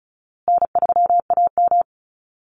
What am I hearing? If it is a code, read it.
Morse code, heard as a keyed tone, about 34 wpm, 700 Hz. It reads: N3AM
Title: Easy Call Signs Repeated 3x - 35wpm